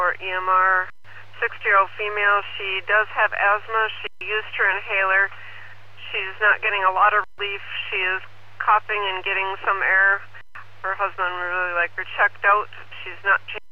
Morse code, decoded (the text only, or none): none